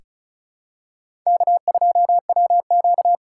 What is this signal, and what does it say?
Morse code, heard as a keyed tone, about 34 wpm, 700 Hz. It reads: K2WQ
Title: USA Call Signs - 4 Characters - 2x Character Spacing - 35wpm